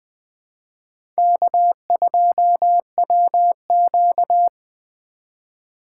Morse code, heard as a keyed tone, about 20 wpm, 700 Hz. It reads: K2WQ
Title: USA Call Signs - 4 Characters - 4x Character Spacing - 20wpm